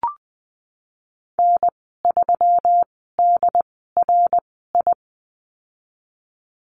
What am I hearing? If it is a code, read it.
Morse code, heard as a keyed tone, about 20 wpm, spaced as for 14 wpm, 700 Hz. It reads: N3DRI
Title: 147 Lesson 24g - Letter y - Call Signs 20wpm